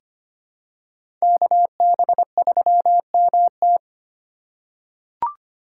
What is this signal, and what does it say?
Morse code, heard as a keyed tone, about 25 wpm, 700 Hz. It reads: KB3MT
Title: USA Call Signs - 5 Characters - 6x Character Spacing - 25wpm